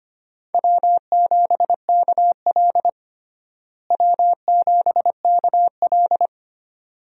Morse code, heard as a keyed tone, about 25 wpm, 700 Hz. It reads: W7KL W7KL
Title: USA Call Signs - 4 Characters - Repeated 2x - 25wpm